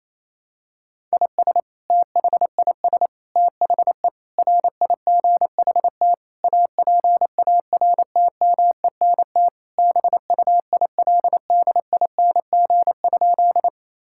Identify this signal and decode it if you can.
Morse code, heard as a keyed tone, about 28 wpm, 700 Hz. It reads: IS THIS THE RIGHT APARTMENT BUILDING?
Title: Sentences from Top 1000 Words - Review Part 6 - 28wpm